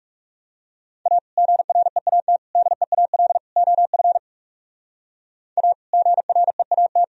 Morse code, heard as a keyed tone, about 39 wpm, 700 Hz. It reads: A GREAT DEAL OF A GREAT